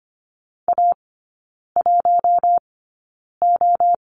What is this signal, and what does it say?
Morse code, heard as a keyed tone, about 25 wpm, spaced as for 9 wpm, 700 Hz. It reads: A1O
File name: Lesson 15e - Number 5 - Three Characters 25wpm